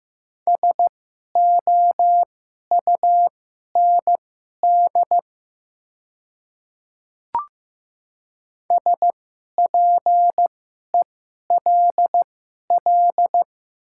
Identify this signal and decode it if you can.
Morse code, heard as a keyed tone, about 15 wpm, spaced as for 11 wpm, 690 Hz. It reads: SOUND SPELL
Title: 2x Character Spacing - Top 500 Words - 15wpm